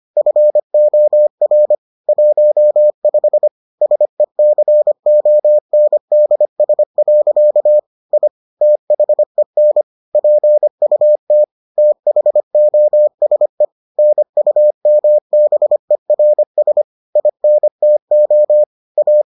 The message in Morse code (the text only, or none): FOR 15 SECONDS. I THEN PUT THOSE NUMBERS INTO A